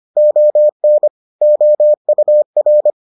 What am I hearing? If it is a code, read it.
Morse code, heard as a keyed tone, about 25 wpm, 600 Hz. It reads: ON OUR